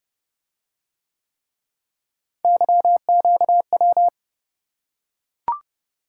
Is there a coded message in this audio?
Yes: Morse code from 2.4 to 4.1 s, heard as a keyed tone, about 28 wpm, 700 Hz. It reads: YQW